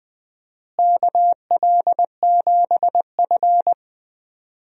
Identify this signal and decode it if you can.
Morse code, heard as a keyed tone, about 20 wpm, 700 Hz. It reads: KL7F
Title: DXCC-100 Call Signs - 4 Characters - Repeated 3x - 20wpm